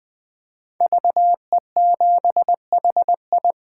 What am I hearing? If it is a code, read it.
Morse code, heard as a keyed tone, about 21 wpm, 700 Hz. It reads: VE7HI
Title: 2023 SOTA Chaser Call Signs - 20wpm